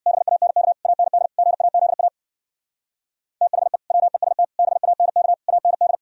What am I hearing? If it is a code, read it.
Morse code, heard as a keyed tone, about 58 wpm, 700 Hz. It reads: BANK AND GAZA THE WEST BANK AND
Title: Speed-Racing - Sets of 5 Words 50wpm